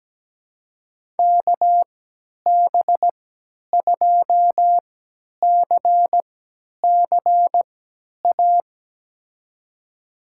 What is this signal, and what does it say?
Morse code, heard as a keyed tone, about 17 wpm, spaced as for 10 wpm, 700 Hz. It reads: KB2CCA
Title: USA Call Signs - 6 Characters - 3x Character Spacing - 17wpm